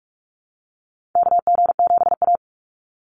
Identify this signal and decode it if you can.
Morse code, heard as a keyed tone, about 44 wpm, 700 Hz. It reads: KG7A
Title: Call Signs - 45wpm